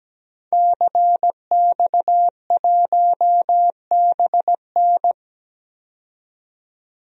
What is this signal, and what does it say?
Morse code, heard as a keyed tone, about 17 wpm, 700 Hz. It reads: CX1BN
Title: DXCC-100 Call Signs - 5 Characters - 17wpm